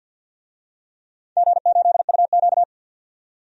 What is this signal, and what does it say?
Morse code, heard as a keyed tone, about 49 wpm, 700 Hz. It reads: G8UQ
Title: DXCC-100 Call Signs - 4 Characters - 50wpm